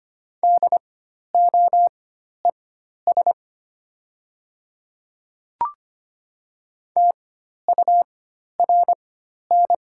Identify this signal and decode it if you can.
Morse code, heard as a keyed tone, about 25 wpm, spaced as for 12 wpm, 700 Hz. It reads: DOES TURN
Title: 4x Character Spacing - Top 200 Words - 25wpm